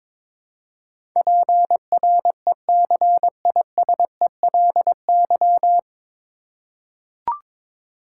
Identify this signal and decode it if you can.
Morse code, heard as a keyed tone, about 22 wpm, 700 Hz. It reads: PRECISELY
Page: Morse Code Ninja practice 22wpm